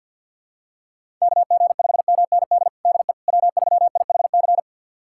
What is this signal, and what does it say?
Morse code, heard as a keyed tone, about 48 wpm, 700 Hz. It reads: KG5MND DE W3IHC
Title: USA Callsign DE Callsign 50wpm